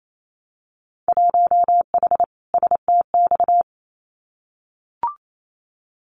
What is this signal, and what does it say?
Morse code, heard as a keyed tone, about 28 wpm, 700 Hz. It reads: 1H STX
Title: Winter Field Day Exchanges Encore 28wpm